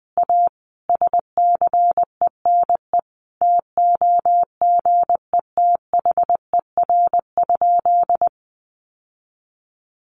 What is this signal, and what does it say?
Morse code, heard as a keyed tone, about 20 wpm, 700 Hz. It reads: A SCENE TOGETHER?
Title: Sentences from Top 1100 Words 20wpm